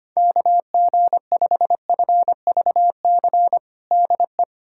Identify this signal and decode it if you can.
Morse code, heard as a keyed tone, about 25 wpm, 700 Hz. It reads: KG5FVC DE
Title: USA Callsign DE Callsign 25wpm